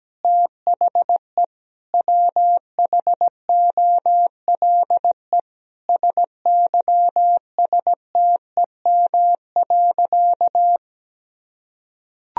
Morse code, heard as a keyed tone, about 17 wpm, 700 Hz. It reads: THE WHOLE SYSTEM.
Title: Sentences from Top 1000 Words - Review Part 3 - 17wpm